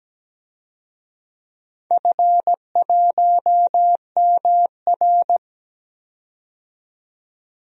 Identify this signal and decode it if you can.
Morse code, heard as a keyed tone, about 17 wpm, 700 Hz. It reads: F1MR
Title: DXCC-100 Call Signs - 4 Characters - 17wpm